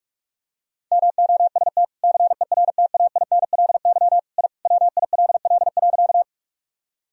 Morse code, heard as a keyed tone, about 46 wpm, 700 Hz. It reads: MOST CERTAINLY I WILL.
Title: Sentences from Top 1000 Words - Review Part 1 - 45wpm